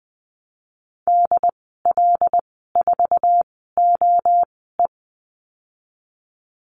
Morse code, heard as a keyed tone, about 19 wpm, spaced as for 14 wpm, 700 Hz. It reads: DL4OE
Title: 088 Lesson 15g - Number 5 - Call Signs 20wpm